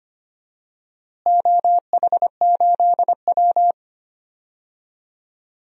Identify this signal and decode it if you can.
Morse code, heard as a keyed tone, about 25 wpm, 700 Hz. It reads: OH8W
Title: DXCC-100 Call Signs - 4 Characters - Repeated 2x - 25wpm